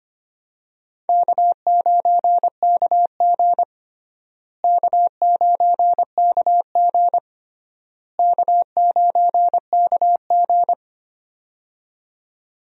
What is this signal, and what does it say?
Morse code, heard as a keyed tone, about 25 wpm, 700 Hz. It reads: K9KG K9KG K9KG
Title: USA Call Signs - 4 Characters - Repeated 3x - 25wpm